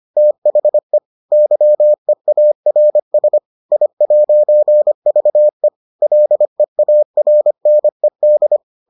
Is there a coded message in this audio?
Yes: Morse code from 0.2 to 8.6 s, heard as a keyed tone, about 25 wpm, 600 Hz. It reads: THE YEARS I'VE LEARNED